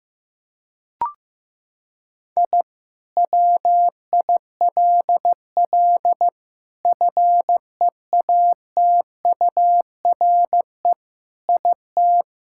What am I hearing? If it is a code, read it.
Morse code, heard as a keyed tone, about 16 wpm, 700 Hz. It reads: I WILL FEATURE IT
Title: Sentences from Top 1000 Words - Review Part 5 - 15wpm